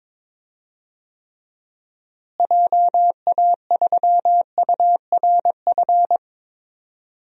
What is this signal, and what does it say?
Morse code, heard as a keyed tone, about 22 wpm, 700 Hz. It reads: JA3URF
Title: DXCC-100 Call Signs - 6 Characters - Repeated 2x - 22wpm